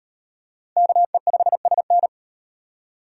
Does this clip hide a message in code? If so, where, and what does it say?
Yes: Morse code from 0.8 to 2.1 s, heard as a keyed tone, about 37 wpm, 700 Hz. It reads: KE5SN